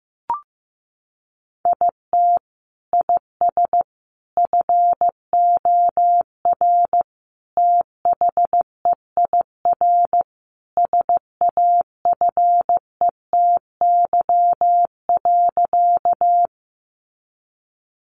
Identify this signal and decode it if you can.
Morse code, heard as a keyed tone, about 15 wpm, 700 Hz. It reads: IT IS FOR THEIR SAFETY.
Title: Sentences from Top 700 Words 15wpm